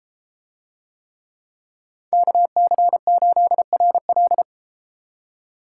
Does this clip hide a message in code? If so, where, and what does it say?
Yes: Morse code from 2.1 to 4.4 s, heard as a keyed tone, about 32 wpm, 700 Hz. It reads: KC8RL